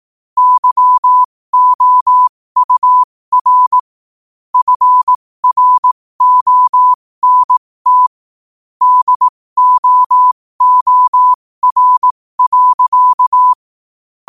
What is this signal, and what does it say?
Morse code, heard as a keyed tone, about 18 wpm, spaced as for 15 wpm, 1000 Hz. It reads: YOUR FRONT DOOR.
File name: qotdfiles / 15wpm